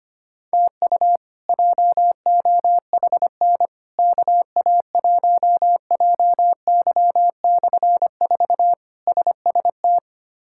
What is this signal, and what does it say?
Morse code, heard as a keyed tone, about 25 wpm, 700 Hz. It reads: TU JOHN KA1JY/4 SST
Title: K1USN SST QSO Practice - 25wpm